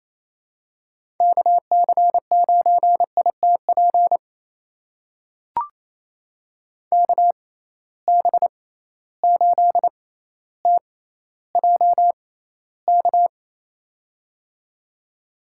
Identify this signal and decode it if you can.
Morse code, heard as a keyed tone, about 28 wpm, spaced as for 14 wpm, 700 Hz. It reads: KC9ITP KB8TJK
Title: USA Call Signs - 6 Characters - 6x Character Spacing - 28wpm